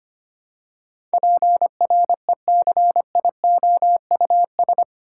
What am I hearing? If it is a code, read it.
Morse code, heard as a keyed tone, about 25 wpm, 700 Hz. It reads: PRECIOUS
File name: 8 Letter Words - First 3 Letters Given - 25wpm